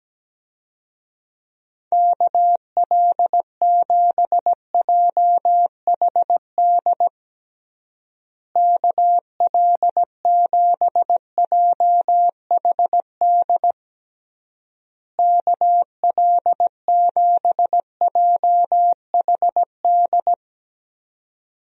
Morse code, heard as a keyed tone, about 17 wpm, 700 Hz. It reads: KL7JHD KL7JHD KL7JHD
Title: USA Call Signs - 6 Characters - Repeated 3x - 17wpm